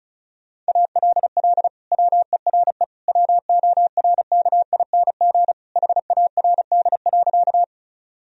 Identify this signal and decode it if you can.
Morse code, heard as a keyed tone, about 34 wpm, 700 Hz. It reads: ALL WERE WORKING HARD.
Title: Sentences from Top 1000 Words - Review Part 4 - 35wpm